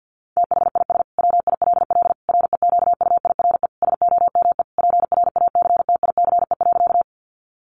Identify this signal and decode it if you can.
Morse code, heard as a keyed tone, about 50 wpm, 700 Hz. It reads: THIS WILL REQUIRE SOME PRACTICE.
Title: Sentences from Top 900 Words 50wpm